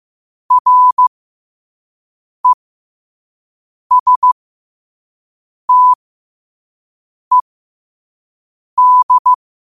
Morse code, heard as a keyed tone, about 15 wpm, spaced as for 5 wpm, 1000 Hz. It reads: RESTED
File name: qotdfiles / 5wpm